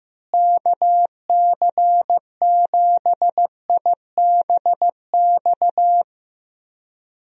Morse code, heard as a keyed tone, about 15 wpm, 700 Hz. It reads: KC7IBX
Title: USA Call Signs - 6 Characters - Repeated 3x - 15wpm